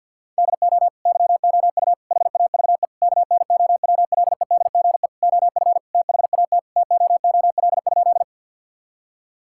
Morse code, heard as a keyed tone, about 51 wpm, 700 Hz. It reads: DO YOU HAVE KNOWLEDGE OF THAT TOOL?
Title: Sentences from Top 1000 Words - Review Part 6 - 50wpm